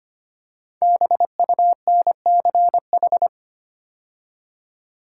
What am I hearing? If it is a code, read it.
Morse code, heard as a keyed tone, about 25 wpm, 700 Hz. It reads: BUNCH
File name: 5 Letter Words - First 3 Letters Given - 25wpm